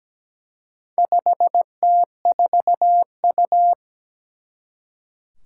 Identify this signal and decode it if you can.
Morse code, heard as a keyed tone, about 17 wpm, 700 Hz. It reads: 5T4U
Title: Four Challenging Letter-Number Combinations 17wpm